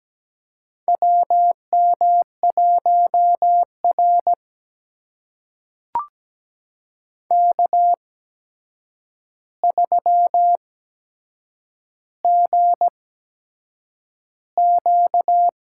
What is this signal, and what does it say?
Morse code, heard as a keyed tone, about 17 wpm, spaced as for 7 wpm, 700 Hz. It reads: WM1R K3GQ
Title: USA Call Signs - 4 Characters - 8x Character Spacing - 17wpm